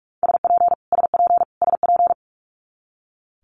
Morse code, heard as a keyed tone, about 46 wpm, 700 Hz. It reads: SP SP SP
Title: Multi-letter Phonemes Repeated 3x 45wpm